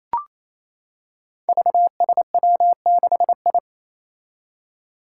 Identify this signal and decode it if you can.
Morse code, heard as a keyed tone, about 27 wpm, 700 Hz. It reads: VSW6I
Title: Five Challenging Letter-Number Combinations 28wpm